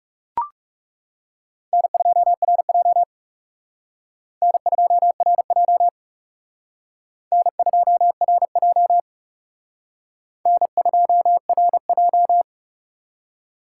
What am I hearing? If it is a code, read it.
Morse code, heard as a keyed tone, about 39 wpm, 700 Hz. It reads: N2RJ N2RJ N2RJ N2RJ